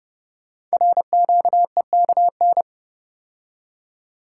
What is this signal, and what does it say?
Morse code, heard as a keyed tone, about 30 wpm, 700 Hz. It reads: RQEKN
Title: Code Groups - 5 Letters - 30wpm